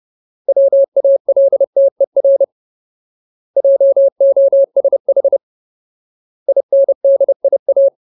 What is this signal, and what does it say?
Morse code, heard as a keyed tone, about 30 wpm, 550 Hz. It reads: WALTER JOSH INDIA
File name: Ultimate Word List / 30WPM Ultimate